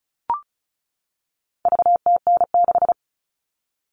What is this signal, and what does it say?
Morse code, heard as a keyed tone, about 34 wpm, 700 Hz. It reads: VTN6
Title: All Cut Numbers - 4-Digit 35wpm